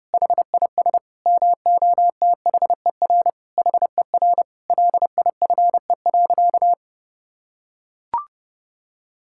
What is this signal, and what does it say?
Morse code, heard as a keyed tone, about 30 wpm, 700 Hz. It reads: HIS MOTHER HER LIFE.
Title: Sentences from Top 1000 Words - Review Part 5 - 30wpm